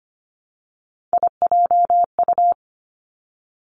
Morse code, heard as a keyed tone, about 25 wpm, 700 Hz. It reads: IJU
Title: Three Challenging Letter-Number Combinations 25wpm